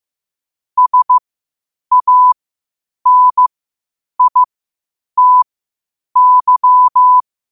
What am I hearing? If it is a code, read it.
Morse code, heard as a keyed tone, about 15 wpm, spaced as for 8 wpm, 1000 Hz. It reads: SANITY
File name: qotdfiles / 8wpm